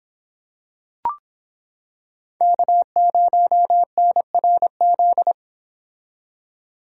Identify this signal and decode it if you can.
Morse code, heard as a keyed tone, about 26 wpm, 700 Hz. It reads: K0NRZ